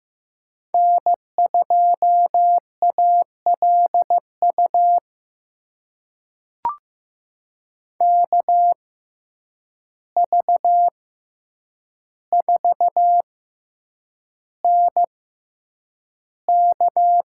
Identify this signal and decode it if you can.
Morse code, heard as a keyed tone, about 15 wpm, spaced as for 8 wpm, 700 Hz. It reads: N2ALU KV4NK
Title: USA Call Signs - 5 Characters - 6x Character Spacing - 15wpm